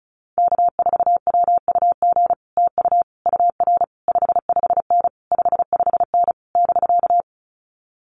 Morse code, heard as a keyed tone, about 34 wpm, 700 Hz. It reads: K4WUG TU UR 55N 55N <BK>
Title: Common QSO Phrases Encore 35wpm